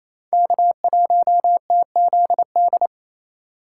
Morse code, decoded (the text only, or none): K1TZD